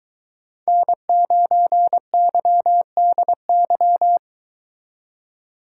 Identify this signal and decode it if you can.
Morse code, heard as a keyed tone, about 23 wpm, 700 Hz. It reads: N9YDY